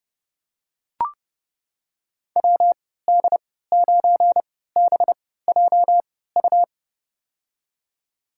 Morse code, heard as a keyed tone, about 32 wpm, spaced as for 17 wpm, 700 Hz. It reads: WD9BJU
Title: USA Call Signs - 6 Characters - 3x Character Spacing - 30wpm